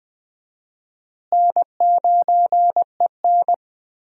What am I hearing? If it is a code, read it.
Morse code, heard as a keyed tone, about 20 wpm, 700 Hz. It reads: N9EN
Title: USA Call Signs - 4 Characters - Repeated 3x - 20wpm